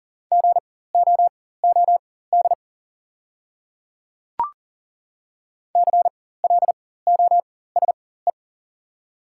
Morse code, heard as a keyed tone, about 40 wpm, spaced as for 19 wpm, 700 Hz. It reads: GOOD CLOSE
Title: 4x Character Spacing - Top 500 Words - 40wpm